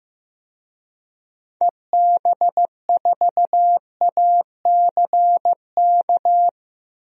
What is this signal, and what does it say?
Morse code, heard as a keyed tone, about 15 wpm, 700 Hz. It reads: EB4ACK
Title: DXCC-100 Call Signs - 6 Characters - 15wpm